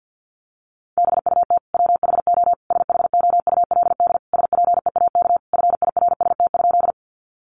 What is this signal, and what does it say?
Morse code, heard as a keyed tone, about 50 wpm, 700 Hz. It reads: BUT WHO SHOULD SPEAK FIRST?
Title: Sentences from Top 1100 Words 50wpm